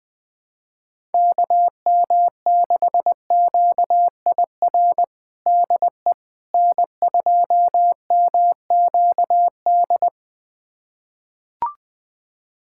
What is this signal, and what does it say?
Morse code, heard as a keyed tone, about 20 wpm, 700 Hz. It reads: KM6QIR DE N2MQD
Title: USA Callsign DE Callsign 20wpm